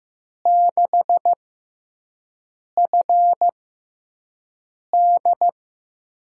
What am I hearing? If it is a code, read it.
Morse code, heard as a keyed tone, about 15 wpm, spaced as for 5 wpm, 700 Hz. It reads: 6FD
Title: USA Call Signs - 6x Character Spacing - 15wpm